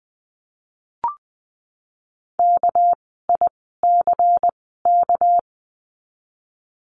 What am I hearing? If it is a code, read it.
Morse code, heard as a keyed tone, about 20 wpm, spaced as for 15 wpm, 700 Hz. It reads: KICK
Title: 199 Lesson 32f - Letter k - Words 20wpm